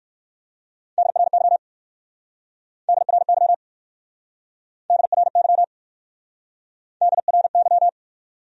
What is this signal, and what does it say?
Morse code, heard as a keyed tone, about 67 wpm, 700 Hz. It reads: DRY DRY DRY DRY